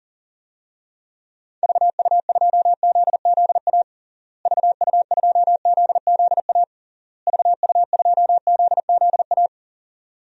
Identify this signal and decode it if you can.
Morse code, heard as a keyed tone, about 37 wpm, 700 Hz. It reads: VU2ZZA VU2ZZA VU2ZZA
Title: DXCC-100 Call Signs - 6 Characters - Repeated 3x - 40wpm